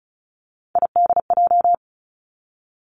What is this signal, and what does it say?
Morse code, heard as a keyed tone, about 36 wpm, 700 Hz. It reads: IDJ
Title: Three Challenging Letter-Number Combinations 35wpm